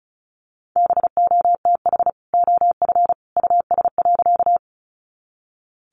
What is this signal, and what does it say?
Morse code, heard as a keyed tone, about 34 wpm, 700 Hz. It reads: BOTH OF US.
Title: Sentences from Top 300 Words 35wpm